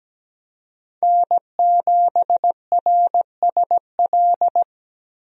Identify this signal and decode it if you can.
Morse code, heard as a keyed tone, about 17 wpm, 700 Hz. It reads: N7RSL
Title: USA Call Signs - 5 Characters - Repeated 3x - 17wpm